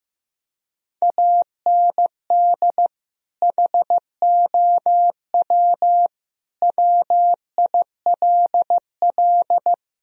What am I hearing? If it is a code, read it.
Morse code, heard as a keyed tone, about 16 wpm, 700 Hz. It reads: AND HOW WILL
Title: Sentences from Top 1000 Words - Review Part 4 - 15wpm